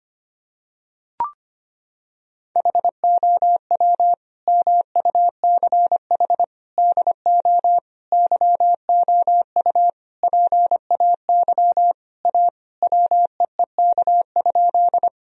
Morse code, heard as a keyed tone, about 25 wpm, 700 Hz. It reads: HOW MUCH DO YOU PAY A WEEK?
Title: Sentences from Top 1000 Words - Review Part 2 - 25wpm